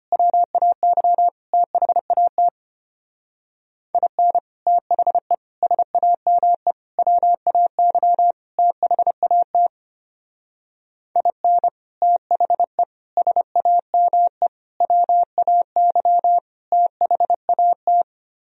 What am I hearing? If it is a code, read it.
Morse code, heard as a keyed tone, about 31 wpm, 700 Hz. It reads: WAY THAT IN THE SAME WAY THAT IN THE SAME WAY THAT
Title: Speed-Racing - Sets of 5 Words 25wpm